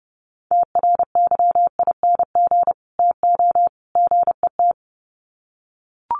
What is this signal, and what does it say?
Morse code, heard as a keyed tone, about 30 wpm, 700 Hz. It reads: TRYING TO GET
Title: Sets of 3 Words 30wpm